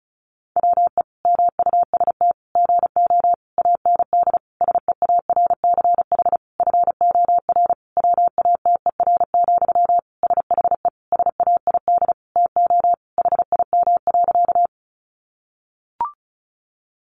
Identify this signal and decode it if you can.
Morse code, heard as a keyed tone, about 34 wpm, 700 Hz. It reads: WE MUST GO AND SEARCH FOR WATER, SHE SAID TO HIM.
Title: Sentences from Top 800 Words 35wpm